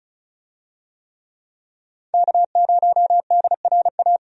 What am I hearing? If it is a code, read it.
Morse code, heard as a keyed tone, about 36 wpm, 700 Hz. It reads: K0DRA
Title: USA Call Signs - 5 Characters - Repeated 2x - 35wpm